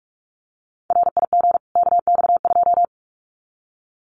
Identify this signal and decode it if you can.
Morse code, heard as a keyed tone, about 44 wpm, 700 Hz. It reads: RIG KX2
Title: Common QSO Phrases 45wpm